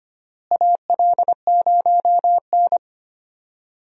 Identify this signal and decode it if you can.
Morse code, heard as a keyed tone, about 25 wpm, 700 Hz. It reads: AL0N
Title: USA Call Signs - 4 Characters - Repeated 3x - 25wpm